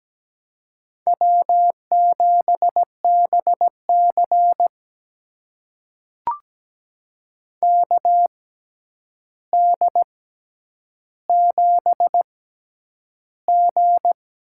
Morse code, heard as a keyed tone, about 17 wpm, spaced as for 9 wpm, 700 Hz. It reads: W7BC KD7G
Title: USA Call Signs - 4 Characters - 6x Character Spacing - 17wpm